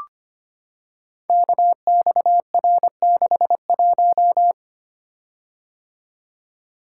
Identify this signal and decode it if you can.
Morse code, heard as a keyed tone, about 25 wpm, 700 Hz. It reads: KXR61
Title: Code Groups - 5 Letter:Numbers - 25wpm